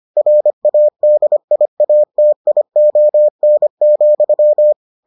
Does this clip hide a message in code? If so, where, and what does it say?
Yes: Morse code from 0.2 to 4.7 s, heard as a keyed tone, about 25 wpm, 600 Hz. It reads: RADIATION,